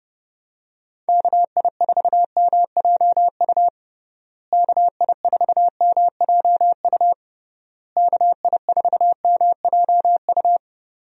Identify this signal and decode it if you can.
Morse code, heard as a keyed tone, about 30 wpm, 700 Hz. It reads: KI4MJU KI4MJU KI4MJU
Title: USA Call Signs - 6 Characters - Repeated 3x - 30wpm